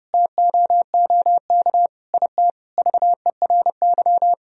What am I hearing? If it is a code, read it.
Morse code, heard as a keyed tone, about 30 wpm, 700 Hz. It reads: TOOK IT VERY
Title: Sentences from Top 1000 Words - Review Part 1 - 30wpm